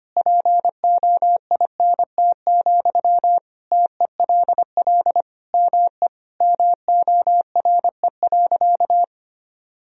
Morse code, heard as a keyed tone, about 25 wpm, 700 Hz. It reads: POINT, TELL ME MORE.
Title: Sentences from Top 1000 Words - Review Part 1 - 25wpm